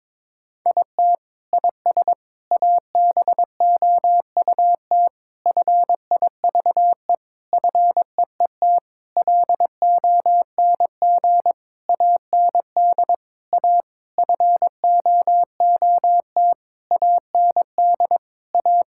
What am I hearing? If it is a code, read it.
Morse code, heard as a keyed tone, about 23 wpm, 700 Hz. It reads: IT IS ABOUT FIVE FEET LONG AND A FOOT AND A
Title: Sentences from Top 1000 Words - Review Part 6 - 22wpm